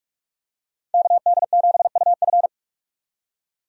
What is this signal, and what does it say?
Morse code, heard as a keyed tone, about 46 wpm, 700 Hz. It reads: KD7UF
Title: USA Call Signs - 5 Characters - 45wpm